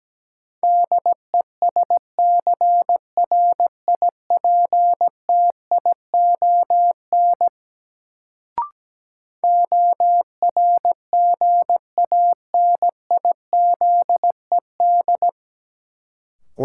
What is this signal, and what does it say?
Morse code, heard as a keyed tone, about 17 wpm, 700 Hz. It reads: DESCRIPTION ORGANIZED
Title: Sentences from Top 1300 Words - New Word Drill - 17wpm